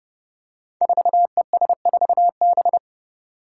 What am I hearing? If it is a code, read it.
Morse code, heard as a keyed tone, about 30 wpm, 700 Hz. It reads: 4ES4B
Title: Five Challenging Letter-Number Combinations 30wpm